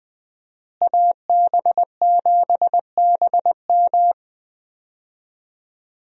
Morse code, heard as a keyed tone, about 20 wpm, 700 Hz. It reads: AB7BM
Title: USA Call Signs - 5 Characters - Repeated 3x - 20wpm